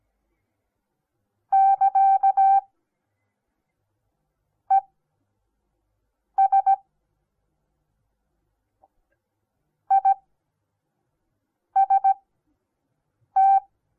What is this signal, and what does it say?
Morse code, heard as a keyed tone, about 17 wpm, spaced as for 5 wpm, 780 Hz. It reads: <KA>ES IST
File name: Onlinelehrgang / morsen / DJ4UF Morsen08